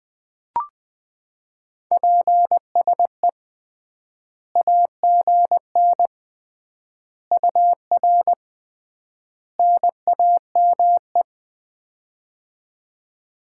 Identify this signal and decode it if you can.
Morse code, heard as a keyed tone, about 20 wpm, 700 Hz. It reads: PSE AGN UR NAME
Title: Common QSO Phrases - 3x Word-Spacing - 20wpm